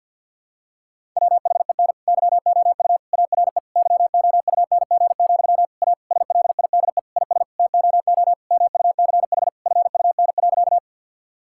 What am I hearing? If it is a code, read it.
Morse code, heard as a keyed tone, about 50 wpm, 700 Hz. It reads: WHEN YOU ARE YOUNG, A SLIDE IS TOO MUCH FUN.
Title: Sentences from Top 1600 Words 50wpm